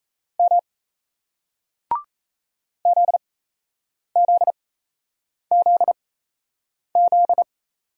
Morse code, heard as a keyed tone, about 39 wpm, spaced as for 8 wpm, 700 Hz. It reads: M ZZZZ